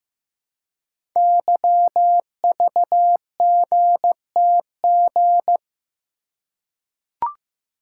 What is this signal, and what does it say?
Morse code, heard as a keyed tone, about 15 wpm, 700 Hz. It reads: YVGTG
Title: Code Groups - 5 Letters - 15wpm